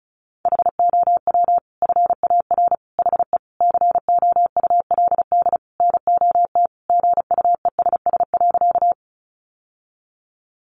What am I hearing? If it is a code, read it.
Morse code, heard as a keyed tone, about 34 wpm, 700 Hz. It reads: HOW FAR HE COULD NOT GUESS.
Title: Sentences from Top 1500 Words 35wpm